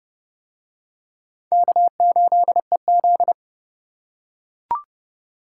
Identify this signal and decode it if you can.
Morse code, heard as a keyed tone, about 29 wpm, 700 Hz. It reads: K8EZ